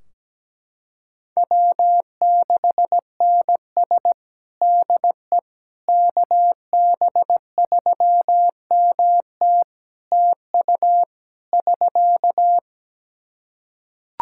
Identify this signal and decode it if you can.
Morse code, heard as a keyed tone, about 17 wpm, 700 Hz. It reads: W6NS DE KB3MT TU <SK>
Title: Practice QSO - Exchange 4 - Ending - 17wpm